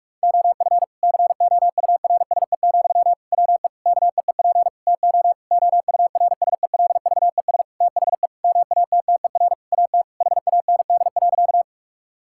Morse code, heard as a keyed tone, about 46 wpm, 700 Hz. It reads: OF COURSE, WE KEEP TO OURSELVES THE MATTER AT HAND.
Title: Sentences from Top 1600 Words 45wpm